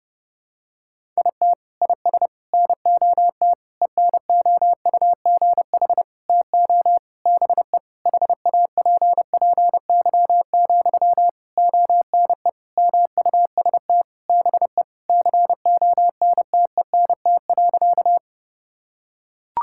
Morse code, heard as a keyed tone, about 32 wpm, 700 Hz. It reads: IT IS NOT ENOUGH TO BE HAPPY, ONE MUST BE CONTENT.
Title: Sentences from Top 1000 Words - Review Part 6 - 30wpm